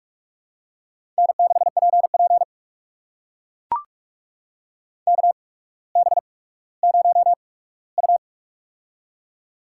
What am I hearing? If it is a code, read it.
Morse code, heard as a keyed tone, about 44 wpm, spaced as for 19 wpm, 700 Hz. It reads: N6PP KB0U
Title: USA Call Signs - 8x Character Spacing - 45wpm